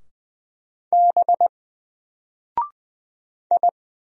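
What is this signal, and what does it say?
Morse code, heard as a keyed tone, about 20 wpm, 700 Hz. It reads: B I
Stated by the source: Single letter and number Morse drill at 20 wpm, from N4LCD.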